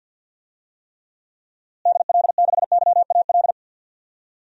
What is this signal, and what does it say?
Morse code, heard as a keyed tone, about 46 wpm, 700 Hz. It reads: DL6YAL
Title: DXCC-100 Call Signs - 6 Characters - Repeated 3x - 50wpm